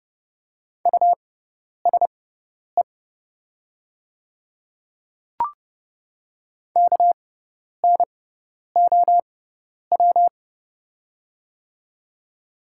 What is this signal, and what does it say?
Morse code, heard as a keyed tone, about 30 wpm, spaced as for 10 wpm, 690 Hz. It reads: USE KNOW
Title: 6x Character Spacing - Top 100 Words - 30wpm